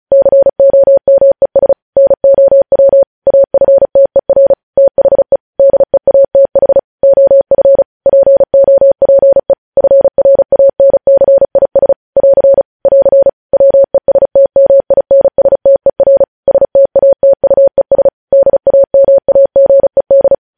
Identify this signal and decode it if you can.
Morse code, heard as a keyed tone, about 36 wpm, 550 Hz. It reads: COMES NOW AFTER THE DEATH OF POPE FRANCIS <AR> <AR> WESTMINSTER STATUES DAMAGED
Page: News Headlines in Morse Code at 35 WPM